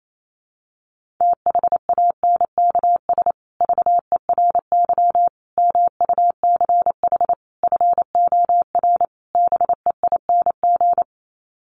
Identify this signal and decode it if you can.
Morse code, heard as a keyed tone, about 28 wpm, 700 Hz. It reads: THANKS VERY MUCH FOR BEING
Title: Sets of 5 Words Encore 28wpm